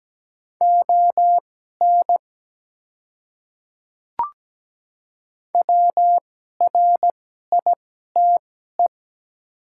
Morse code, heard as a keyed tone, about 17 wpm, spaced as for 12 wpm, 690 Hz. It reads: ON WRITE
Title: 2x Character Spacing - Top 100 Words - 17wpm